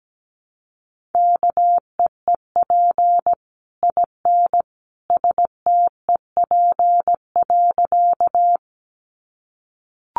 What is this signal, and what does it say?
Morse code, heard as a keyed tone, about 17 wpm, 700 Hz. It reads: KEEP IN STEP.
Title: Sentences from Top 400 Words 17wpm